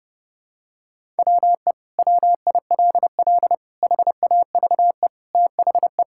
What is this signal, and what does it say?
Morse code, heard as a keyed tone, about 30 wpm, 700 Hz. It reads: WE WILL HAVE THE
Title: Sentences from Top 1000 Words - Review Part 6 - 30wpm